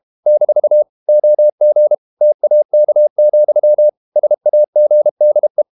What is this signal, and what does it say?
Morse code, heard as a keyed tone, about 32 wpm, 600 Hz. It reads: - OG TAK, SAGDE
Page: Ferie 32 wpm